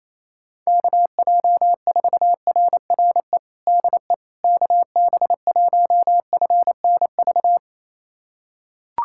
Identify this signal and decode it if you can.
Morse code, heard as a keyed tone, about 28 wpm, 700 Hz. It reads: KJ4RRE DE KB1FNV
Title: USA Callsign DE Callsign - 6 Characters - 28wpm